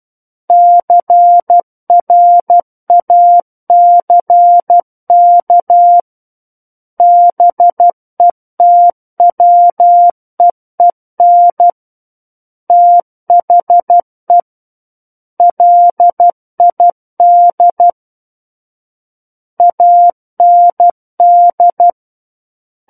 Morse code, heard as a keyed tone, about 12 wpm, 700 Hz. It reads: CRACK BETWEEN THE LID AND